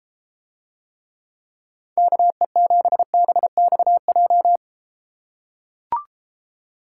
Morse code, heard as a keyed tone, about 33 wpm, 700 Hz. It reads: KE7BXJ